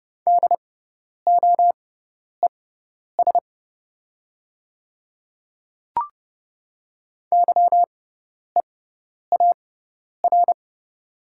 Morse code, heard as a keyed tone, about 30 wpm, spaced as for 10 wpm, 700 Hz. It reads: DOES YEAR
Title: 6x Character Spacing - Top 300 Words - 30wpm